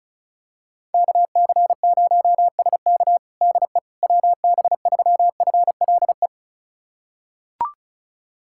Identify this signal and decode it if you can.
Morse code, heard as a keyed tone, about 34 wpm, 700 Hz. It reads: KC0SK DE WB3FLE
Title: USA Callsign DE Callsign 35wpm